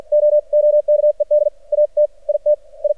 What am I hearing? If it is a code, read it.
Morse code, heard as a keyed tone, about 50 wpm, spaced as for 43 wpm, 600 Hz. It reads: OOKED AT IT I
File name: archived / CRSnet